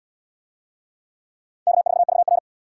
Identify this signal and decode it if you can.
Morse code, heard as a keyed tone, about 72 wpm, 700 Hz. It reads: K4VR